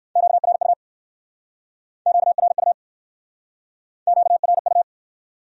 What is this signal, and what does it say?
Morse code, heard as a keyed tone, about 56 wpm, 700 Hz. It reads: QRU QRU QRU
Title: Speed-Racing Q-Codes - 45wpm